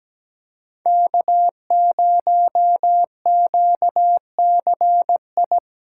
Morse code, heard as a keyed tone, about 17 wpm, 700 Hz. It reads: K0QCI
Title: USA Callsign DE Callsign - 5 Characters - 17wpm